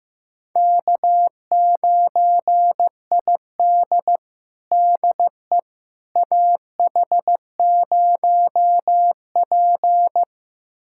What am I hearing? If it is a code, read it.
Morse code, heard as a keyed tone, about 15 wpm, 700 Hz. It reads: K9ID DE AH0P
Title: USA Callsign DE Callsign - 4 Characters - 15wpm